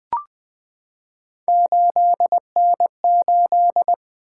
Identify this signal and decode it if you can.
Morse code, heard as a keyed tone, about 20 wpm, 700 Hz. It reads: 8N8
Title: Three Letter-Number Combinations - 20wpm